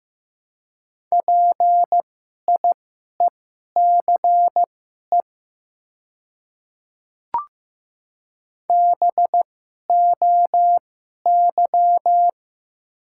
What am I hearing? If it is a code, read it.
Morse code, heard as a keyed tone, about 15 wpm, spaced as for 11 wpm, 700 Hz. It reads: PIECE BOY
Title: 2x Character Spacing - Top 500 Words - 15wpm